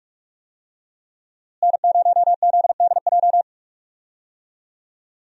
Morse code, heard as a keyed tone, about 46 wpm, 700 Hz. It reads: N0ZDJ
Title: USA Call Signs - 5 Characters - Repeated 2x - 45wpm